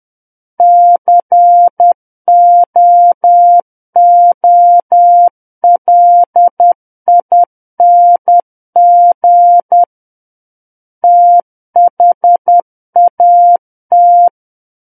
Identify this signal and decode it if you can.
Morse code, heard as a keyed tone, about 10 wpm, 700 Hz. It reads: COOLING THAT
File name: Book One / 01-Chapter 1 - 10 WPM